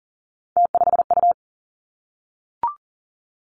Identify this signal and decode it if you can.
Morse code, heard as a keyed tone, about 39 wpm, 700 Hz. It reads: T5U
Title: Three Challenging Letter-Number Combinations 40wpm